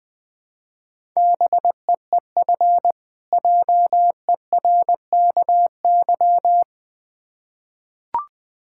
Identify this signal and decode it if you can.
Morse code, heard as a keyed tone, about 20 wpm, 700 Hz. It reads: BEEF JERKY
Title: Difficult Words with Hints - 20wpm